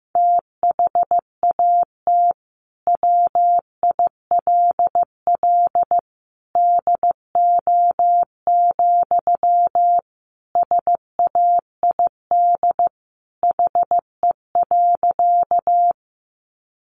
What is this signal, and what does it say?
Morse code, heard as a keyed tone, about 15 wpm, 700 Hz. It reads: THAT WILL DO, SAID HE.
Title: Sentences from Top 100 Words 15wpm